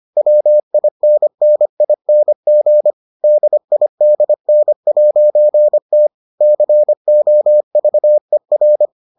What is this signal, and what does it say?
Morse code, heard as a keyed tone, about 25 wpm, 600 Hz. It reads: WINNING DIDN'T COVER